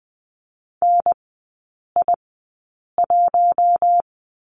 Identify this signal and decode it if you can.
Morse code, heard as a keyed tone, about 20 wpm, spaced as for 8 wpm, 700 Hz. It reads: NI1
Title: 041 Lesson 8e - Number 1 - Three Characters 20wpm